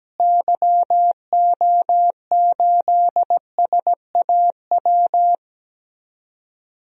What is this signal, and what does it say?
Morse code, heard as a keyed tone, about 17 wpm, 700 Hz. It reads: YO8SAW
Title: DXCC-100 Call Signs - 6 Characters - Repeated 3x - 17wpm